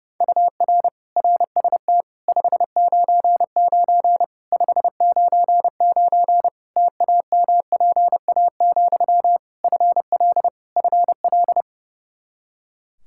Morse code, heard as a keyed tone, about 30 wpm, 700 Hz. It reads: UR RST 599 599 TAMPA, FL FL
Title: K1USN SST QSO Practice 30wpm